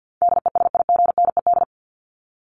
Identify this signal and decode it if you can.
Morse code, heard as a keyed tone, about 50 wpm, 700 Hz. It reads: DESIGNED
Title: ED Suffix - Top 100 Words with Base Words 50wpm